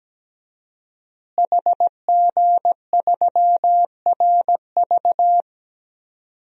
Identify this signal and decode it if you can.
Morse code, heard as a keyed tone, about 17 wpm, 700 Hz. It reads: HG3RV
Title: DXCC-100 Call Signs - 5 Characters - Repeated 2x - 17wpm